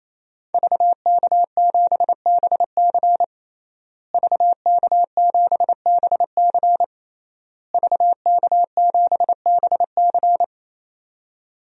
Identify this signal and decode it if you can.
Morse code, heard as a keyed tone, about 27 wpm, 700 Hz. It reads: VK7BC VK7BC VK7BC
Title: DXCC-100 Call Signs - 5 Characters - Repeated 3x - 28wpm